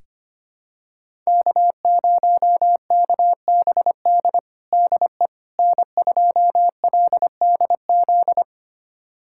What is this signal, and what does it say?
Morse code, heard as a keyed tone, about 25 wpm, 700 Hz. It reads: K0KBD DE N2LDZ
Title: USA Callsign DE Callsign - 5 Characters - 25wpm